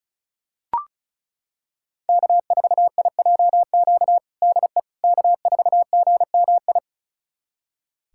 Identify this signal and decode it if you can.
Morse code, heard as a keyed tone, about 36 wpm, 700 Hz. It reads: K4IJQ DE K4GMI
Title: USA Callsign DE Callsign 35wpm